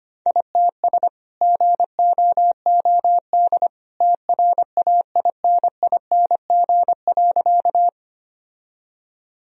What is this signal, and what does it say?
Morse code, heard as a keyed tone, about 25 wpm, 700 Hz. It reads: ITS GOOD TRAINING.
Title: Sentences from Top 1000 Words - Review Part 4 - 25wpm